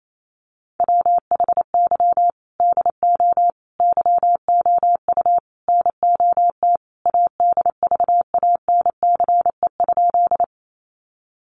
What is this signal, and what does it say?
Morse code, heard as a keyed tone, about 28 wpm, 700 Hz. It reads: WHY DO YOU NOT ADVANCE?
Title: Sentences from Top 1400 Words 28wpm